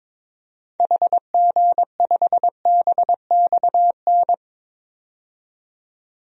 Morse code, heard as a keyed tone, about 22 wpm, 700 Hz. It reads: HG5BXN
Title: DXCC-100 Call Signs - 6 Characters - 22wpm